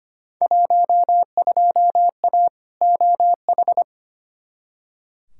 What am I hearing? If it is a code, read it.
Morse code, heard as a keyed tone, about 25 wpm, 700 Hz. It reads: 12A OH
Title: Field Day Exchanges Encore - 25wpm